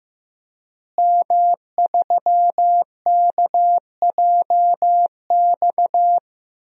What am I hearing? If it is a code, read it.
Morse code, heard as a keyed tone, about 15 wpm, 700 Hz. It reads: M3KJX
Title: DXCC-100 Call Signs - 5 Characters - Repeated 3x - 15wpm